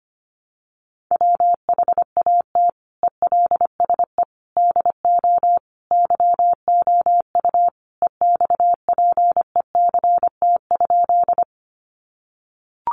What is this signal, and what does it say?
Morse code, heard as a keyed tone, about 25 wpm, 700 Hz. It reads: WHAT ELSE DO YOU EXPECT?
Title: Sentences from Top 1100 Words 25wpm